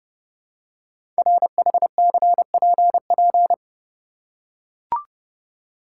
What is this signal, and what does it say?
Morse code, heard as a keyed tone, about 30 wpm, 700 Hz. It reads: RHCPP
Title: Code Groups - 5 Letters - 30wpm